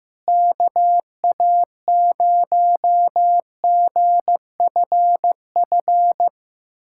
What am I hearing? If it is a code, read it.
Morse code, heard as a keyed tone, about 15 wpm, 700 Hz. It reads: KA0GFF
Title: USA Callsign DE Callsign 15wpm